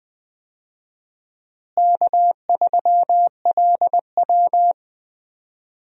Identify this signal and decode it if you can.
Morse code, heard as a keyed tone, about 20 wpm, 700 Hz. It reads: K3LW
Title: USA Call Signs - 4 Characters - Repeated 3x - 20wpm